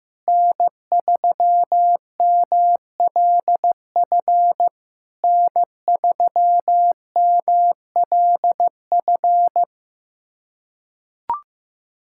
Morse code, heard as a keyed tone, about 15 wpm, 700 Hz. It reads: N3MLF N3MLF
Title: SOTA QSO Practice 15wpm